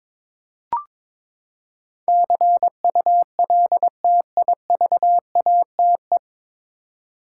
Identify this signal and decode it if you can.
Morse code, heard as a keyed tone, about 22 wpm, 700 Hz. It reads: CULTIVATE
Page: Morse Code Ninja practice 22wpm